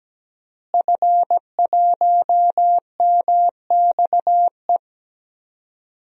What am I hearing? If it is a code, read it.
Morse code, heard as a keyed tone, about 17 wpm, 700 Hz. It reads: F1MXE
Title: DXCC-100 Call Signs - 5 Characters - Repeated 3x - 17wpm